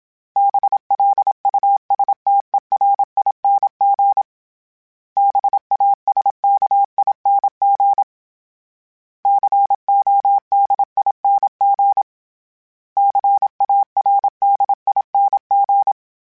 Morse code, heard as a keyed tone, about 27 wpm, 800 Hz. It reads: BLUSTERING BASKING CODING CARDING
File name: Special INGs-mp3s / Special INGs-mp3s / Sngl Ing @25